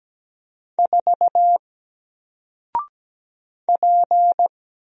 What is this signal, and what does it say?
Morse code, heard as a keyed tone, about 17 wpm, 700 Hz. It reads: 4 P
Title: Single Character (Letter:Number:Punctuation:Prosign) 17wpm